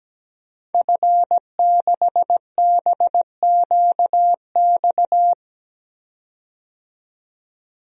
Morse code, heard as a keyed tone, about 17 wpm, 700 Hz. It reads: F6BQX
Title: DXCC-100 Call Signs - 5 Characters - 17wpm